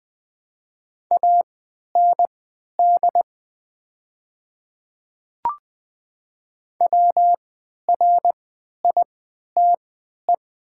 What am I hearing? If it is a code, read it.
Morse code, heard as a keyed tone, about 20 wpm, spaced as for 11 wpm, 700 Hz. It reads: AND WRITE
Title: 3x Character Spacing - Top 100 Words - 20wpm